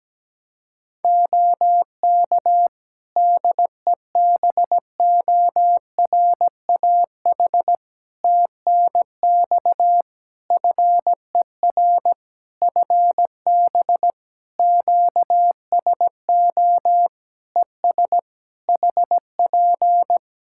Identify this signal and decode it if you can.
Morse code, heard as a keyed tone, about 17 wpm, 700 Hz. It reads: OK DEBORAH TNX FER FB QSO ES HP
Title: Practice QSO - Exchange 4 - Ending - 17wpm